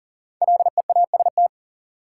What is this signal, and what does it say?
Morse code, heard as a keyed tone, about 40 wpm, 700 Hz. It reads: LEAST
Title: Sentences from Top 700 Words - New Word Drill - 40wpm